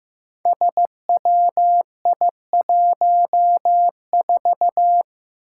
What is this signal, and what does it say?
Morse code, heard as a keyed tone, about 15 wpm, 700 Hz. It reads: SWI14
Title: Five Challenging Letter-Number Combinations 15wpm